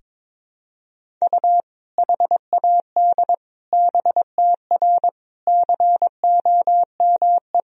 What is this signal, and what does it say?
Morse code, heard as a keyed tone, about 22 wpm, 700 Hz. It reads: U HAD BTR COME
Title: Sentences from Top 400 Words and Abbreviations 22wpm